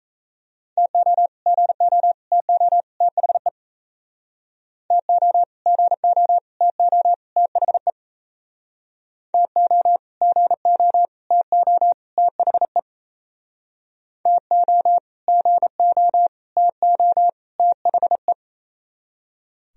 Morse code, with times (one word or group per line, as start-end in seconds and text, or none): 0.8-1.3 TO
1.5-2.1 GO
2.3-2.8 TO
3.0-3.5 THE
4.9-5.4 TO
5.7-6.4 GO
6.6-7.1 TO
7.4-7.9 THE
9.3-10.0 TO
10.2-11.1 GO
11.3-11.9 TO
12.2-12.8 THE
14.3-15.0 TO
15.3-16.3 GO
16.6-17.3 TO
17.6-18.3 THE